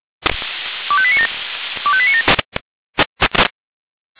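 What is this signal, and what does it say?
Five-tone selective call (ZVEI), heard as quick runs of five tones, tones of about 70 ms. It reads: 25897 25897
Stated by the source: ZVEI five-tone selective call 25447, received by SDR.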